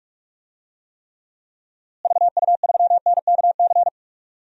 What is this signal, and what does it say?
Morse code, heard as a keyed tone, about 46 wpm, 700 Hz. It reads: VU3NKC
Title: DXCC-100 Call Signs - 6 Characters - Repeated 3x - 45wpm